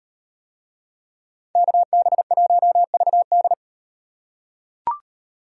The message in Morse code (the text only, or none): KB1VD